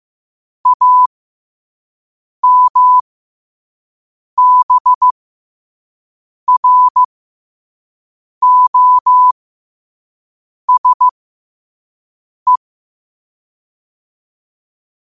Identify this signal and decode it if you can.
Morse code, heard as a keyed tone, about 15 wpm, spaced as for 5 wpm, 1000 Hz. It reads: AMBROSE
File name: qotdfiles / 5wpm